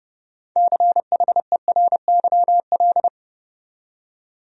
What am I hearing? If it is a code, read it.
Morse code, heard as a keyed tone, about 30 wpm, 700 Hz. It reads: CHERYL
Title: USA Common Names - 30wpm